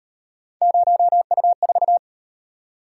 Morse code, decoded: 0U4

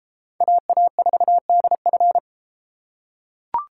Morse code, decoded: AA4DF

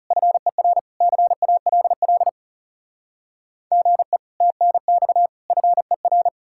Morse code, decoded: FER CALL GE TNX FER

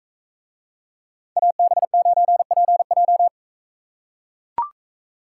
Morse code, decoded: AB9PJ

